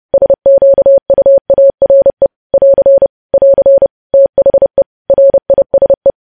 Morse code, heard as a keyed tone, about 30 wpm, 550 Hz. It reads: SQUARE <AR> <AR> THE RISE
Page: news headlines in morse code at 30 wpm